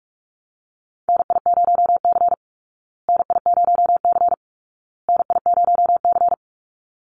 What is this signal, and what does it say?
Morse code, heard as a keyed tone, about 44 wpm, 700 Hz. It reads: NI0C NI0C NI0C
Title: Call Signs Repeated 2x - 45wpm